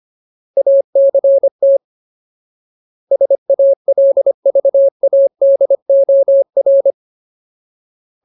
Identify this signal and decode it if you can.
Morse code, heard as a keyed tone, about 25 wpm, 550 Hz. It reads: ACT SALVADOR
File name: Ultimate Word List / 25WPM Ultimate